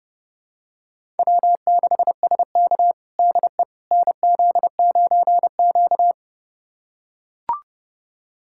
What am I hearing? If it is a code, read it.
Morse code, heard as a keyed tone, about 30 wpm, 700 Hz. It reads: W6SK DE NZ9Q
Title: USA Callsign DE Callsign - 4 Characters - 30wpm